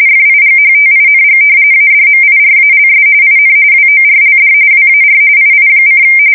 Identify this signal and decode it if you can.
RTTY, about 45.45 baud. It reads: FIGHTING ADMIRALS IN WORLD WAR II:
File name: fsk170 / battleshipnj